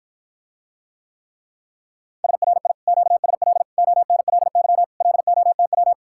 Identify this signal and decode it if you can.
Morse code, heard as a keyed tone, about 53 wpm, 700 Hz. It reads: SRI QSL ONLY LOTW